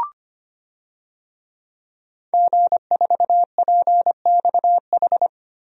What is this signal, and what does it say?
Morse code, heard as a keyed tone, about 25 wpm, 700 Hz. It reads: G4PXH
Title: DXCC-100 Call Signs - 5 Characters - 25wpm